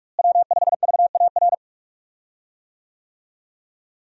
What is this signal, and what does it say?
Morse code, heard as a keyed tone, about 46 wpm, 700 Hz. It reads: W5VAR
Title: USA Call Signs - 5 Characters - Repeated 3x - 45wpm